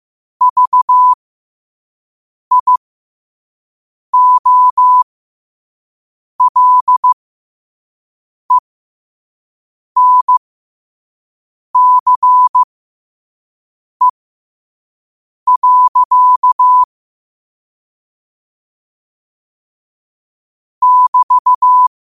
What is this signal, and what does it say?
Morse code, heard as a keyed tone, about 15 wpm, spaced as for 5 wpm, 1000 Hz. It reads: VIOLENCE. <BT>
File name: qotdfiles / 5wpm